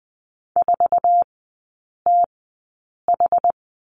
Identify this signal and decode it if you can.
Morse code, heard as a keyed tone, about 20 wpm, spaced as for 8 wpm, 700 Hz. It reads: 4TH
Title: 087 Lesson 15e - Number 5 - Three Characters 20wpm